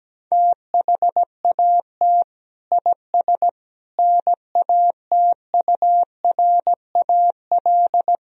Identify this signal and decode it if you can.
Morse code, heard as a keyed tone, about 17 wpm, 700 Hz. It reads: THAT IS NATURAL
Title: Sentences from Top 1000 Words - Review Part 4 - 17wpm